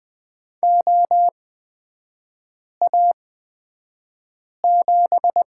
Morse code, heard as a keyed tone, about 20 wpm, spaced as for 5 wpm, 700 Hz. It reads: OA7
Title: 229 Lesson 37e - Letter x - Three Characters 20wpm